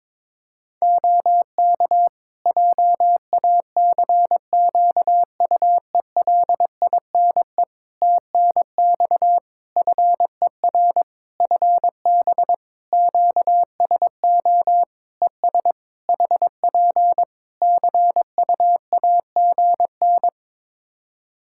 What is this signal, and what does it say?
Morse code, heard as a keyed tone, about 22 wpm, 700 Hz. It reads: OK JACQUELINE TNX FER FB QSO ES HP CUAGN
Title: K1USN SST QSO Practice 22wpm